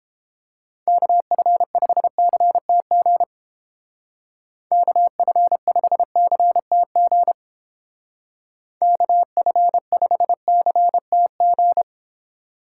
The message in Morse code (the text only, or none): KF5CTG KF5CTG KF5CTG